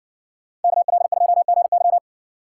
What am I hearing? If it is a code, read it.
Morse code, heard as a keyed tone, about 60 wpm, 700 Hz. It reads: KL2GY